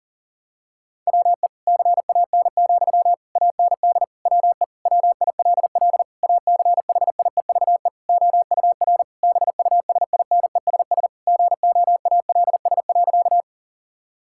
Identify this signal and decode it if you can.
Morse code, heard as a keyed tone, about 40 wpm, 690 Hz. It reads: WE CAN, AND WE WILL ACHIEVE OUR BUSINESS GOALS.
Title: Sentences from Top 1000 Words - Review Part 6 - 40wpm